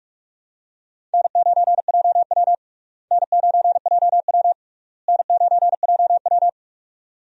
Morse code, heard as a keyed tone, about 41 wpm, 700 Hz. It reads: N9JW N9JW N9JW
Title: USA Call Signs - 4 Characters - Repeated 3x - 45wpm